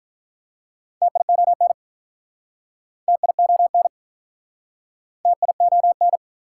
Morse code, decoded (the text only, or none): TION TION TION